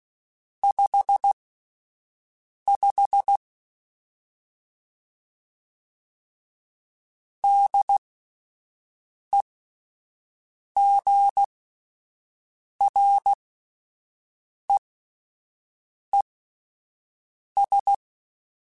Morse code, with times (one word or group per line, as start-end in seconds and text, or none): 0.6-3.4 55
7.4-17.9 DEGREES